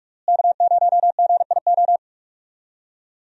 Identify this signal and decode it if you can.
Morse code, heard as a keyed tone, about 41 wpm, 700 Hz. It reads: K0GIO
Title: USA Call Signs - 5 Characters - Repeated 3x - 45wpm